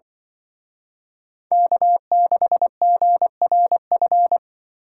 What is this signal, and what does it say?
Morse code, heard as a keyed tone, about 25 wpm, 700 Hz. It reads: K6GRF
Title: Speed Racing - USA Call Signs 20wpm